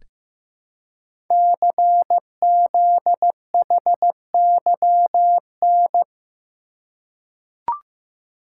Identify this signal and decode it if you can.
Morse code, heard as a keyed tone, about 15 wpm, 700 Hz. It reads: CZHYN
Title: Code Groups - 5 Letters - 15wpm